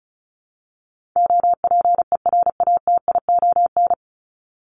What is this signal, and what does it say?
Morse code, heard as a keyed tone, about 34 wpm, 700 Hz. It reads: OPERATION
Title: Top 1501-1750 Words 35wpm